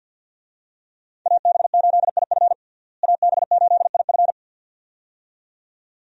Morse code, heard as a keyed tone, about 51 wpm, 700 Hz. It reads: AB8IF AB8IF
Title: USA Call Signs - 5 Characters - Repeated 2x - 50wpm